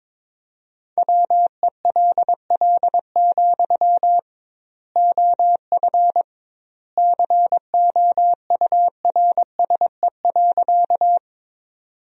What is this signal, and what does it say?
Morse code, heard as a keyed tone, about 22 wpm, 700 Hz. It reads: WELL, OF COURSE.
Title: 2x Word-Spacing - Sentences from Top 500 Words - 22wpm